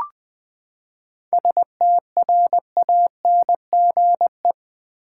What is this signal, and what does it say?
Morse code, heard as a keyed tone, about 19 wpm, 700 Hz. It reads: STRANGE
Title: Seven Letter Words - 20wpm